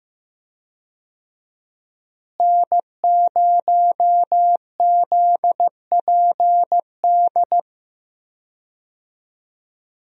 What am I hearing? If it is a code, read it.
Morse code, heard as a keyed tone, about 15 wpm, 700 Hz. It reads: N0ZPD
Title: USA Call Signs - 5 Characters - Repeated 3x - 15wpm